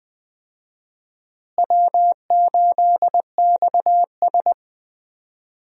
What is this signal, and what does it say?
Morse code, heard as a keyed tone, about 20 wpm, 700 Hz. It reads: W8XS
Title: USA Call Signs - 4 Characters - Repeated 3x - 20wpm